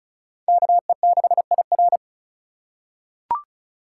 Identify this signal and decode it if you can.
Morse code, heard as a keyed tone, about 36 wpm, 700 Hz. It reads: KE6IR
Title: USA Call Signs - 5 Characters - 4x Character Spacing - 35wpm